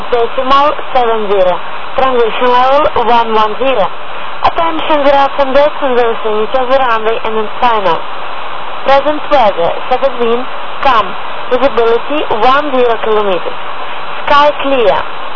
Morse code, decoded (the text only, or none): none